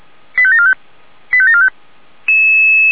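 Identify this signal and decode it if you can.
Five-tone selective call (ZVEI), heard as quick runs of five tones, tones of about 70 ms. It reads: 85745 85745
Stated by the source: ZVEI five-tone selective call received on an SDR.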